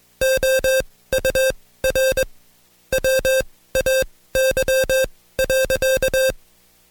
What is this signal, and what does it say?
Morse code, heard as a keyed tone, about 21 wpm, spaced as for 16 wpm, 530 Hz. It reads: OUR WAY.
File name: Code Quick / AudioDownld2 / Seg134-Stepper7 15WPM